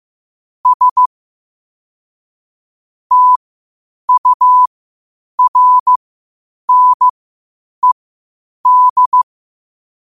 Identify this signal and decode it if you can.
Morse code, heard as a keyed tone, about 15 wpm, spaced as for 8 wpm, 1000 Hz. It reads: S TURNED
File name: qotdfiles / 8wpm